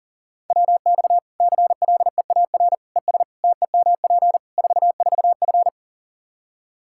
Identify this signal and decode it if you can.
Morse code, heard as a keyed tone, about 40 wpm, 700 Hz. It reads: WX CLEAR ES TEMP 44F
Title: Practice QSO - Exchange 2 - Info - 40wpm